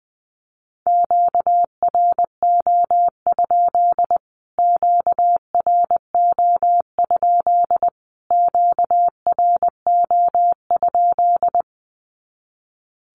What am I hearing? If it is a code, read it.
Morse code, heard as a keyed tone, about 20 wpm, 700 Hz. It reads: QRO? QRO? QRO?
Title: Q-Codes Repeated 3x 20wpm